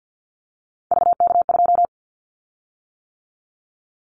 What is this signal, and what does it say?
Morse code, heard as a keyed tone, about 50 wpm, 700 Hz. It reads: VK2
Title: Callsign Prefixes for Top 50 DXCC - 50wpm